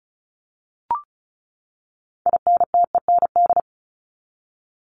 Morse code, heard as a keyed tone, about 34 wpm, 700 Hz. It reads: INTEND
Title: Top 1501-1750 Words 35wpm